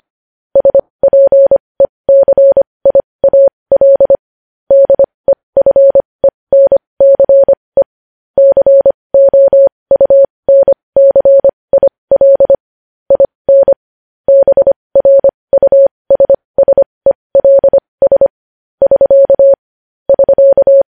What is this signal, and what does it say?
Morse code, heard as a keyed tone, about 25 wpm, spaced as for 20 wpm, 550 Hz. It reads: SPECIAL DEFENCE COUNCIL IN BRUSSELS <SK> <SK>
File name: podcast / News headlines-1x-25x20